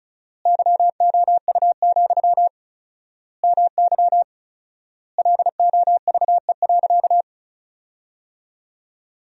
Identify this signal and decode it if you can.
Morse code, heard as a keyed tone, about 34 wpm, 700 Hz. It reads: YOU, MY LOVE.
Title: 4x Word-Spacing - Sentences from Top 400 Words - 35wpm